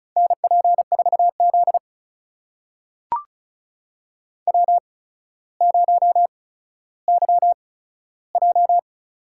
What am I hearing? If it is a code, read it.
Morse code, heard as a keyed tone, about 36 wpm, spaced as for 15 wpm, 700 Hz. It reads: NP4Z W0YJ
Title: USA Call Signs - 8x Character Spacing - 35wpm